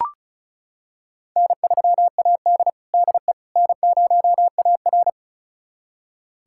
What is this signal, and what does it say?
Morse code, heard as a keyed tone, about 36 wpm, 700 Hz. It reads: N3AD DE N0AR
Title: USA Callsign DE Callsign - 4 Characters - 35wpm